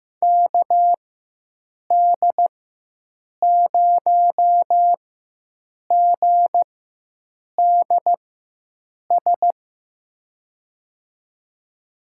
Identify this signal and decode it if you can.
Morse code, heard as a keyed tone, about 15 wpm, spaced as for 7 wpm, 700 Hz. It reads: KD0GDS
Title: USA Call Signs - 6 Characters - 4x Character Spacing - 15wpm